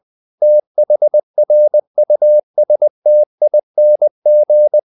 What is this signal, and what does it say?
Morse code, heard as a keyed tone, about 20 wpm, 600 Hz. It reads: THRUSTING
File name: the old man and the sea part 2 20wpm